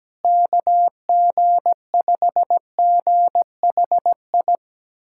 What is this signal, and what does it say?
Morse code, heard as a keyed tone, about 17 wpm, 700 Hz. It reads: KG5GHI
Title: USA Callsign DE Callsign 17wpm